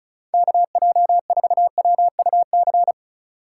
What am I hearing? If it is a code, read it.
Morse code, heard as a keyed tone, about 36 wpm, 700 Hz. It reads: KJ4WUC
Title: USA Call Signs - 6 Characters - 2x Character Spacing - 35wpm